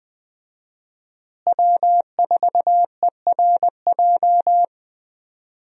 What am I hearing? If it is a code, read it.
Morse code, heard as a keyed tone, about 20 wpm, 700 Hz. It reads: W4ERJ
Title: USA Call Signs - 5 Characters - Repeated 3x - 20wpm